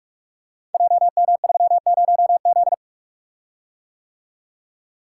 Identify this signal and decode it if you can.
Morse code, heard as a keyed tone, about 46 wpm, 700 Hz. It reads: JM30Z
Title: Code Groups - 5 Letter:Numbers - 45wpm